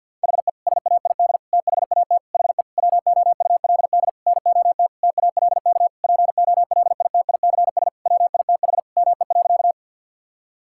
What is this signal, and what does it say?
Morse code, heard as a keyed tone, about 52 wpm, 700 Hz. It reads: HE SAID THAT HE WOULD NOT TALK POLITICS WITH ME.
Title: Sentences from Top 1000 Words - Review Part 6 - 50wpm